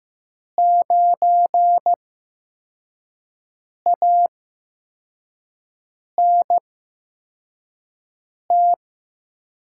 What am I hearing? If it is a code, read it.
Morse code, heard as a keyed tone, about 15 wpm, spaced as for 4 wpm, 700 Hz. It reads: 9ANT
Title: USA Call Signs - 8x Character Spacing - 15wpm